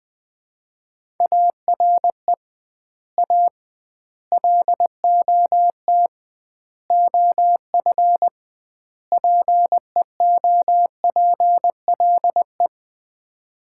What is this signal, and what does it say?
Morse code, heard as a keyed tone, about 21 wpm, 700 Hz. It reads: ARE A LOT OF PEOPLE
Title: 2x Word-Spacing - Sets of 5 Words - 20wpm